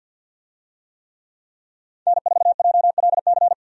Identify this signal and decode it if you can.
Morse code, heard as a keyed tone, about 47 wpm, 700 Hz. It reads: N4JLC
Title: USA Call Signs - 5 Characters - 50wpm